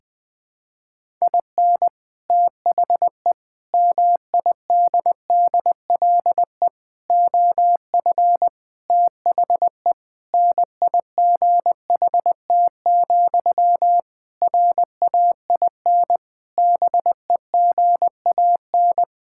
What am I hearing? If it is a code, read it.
Morse code, heard as a keyed tone, about 19 wpm, 700 Hz. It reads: IN THE MIDDLE OF THE NIGHT, RAIN BEGAN
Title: Sentences from Top 1000 Words - Review Part 6 - 20wpm